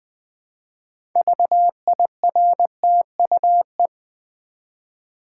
Morse code, heard as a keyed tone, about 20 wpm, 700 Hz. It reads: VIRTUE
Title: 6 Letter Words - First 3 Letters Given - 20wpm